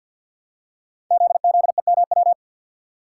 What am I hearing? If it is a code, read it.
Morse code, heard as a keyed tone, about 48 wpm, 700 Hz. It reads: ZZEMW
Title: Code Groups - 5 Letters - 50wpm